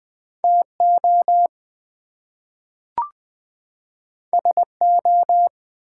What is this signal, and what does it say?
Morse code, heard as a keyed tone, about 20 wpm, 700 Hz. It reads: TO SO
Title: Two Letter Words - 20wpm